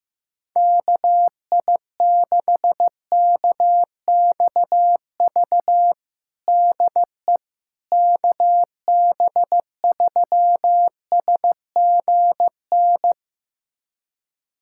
Morse code, heard as a keyed tone, about 16 wpm, 700 Hz. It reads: KI6KXV DE KB3SGN
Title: USA Callsign DE Callsign 15wpm